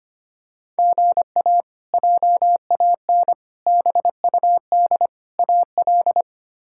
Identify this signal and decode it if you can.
Morse code, heard as a keyed tone, about 25 wpm, 700 Hz. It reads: GA JAN BUD AL
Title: K1USN SST QSO Practice - 25wpm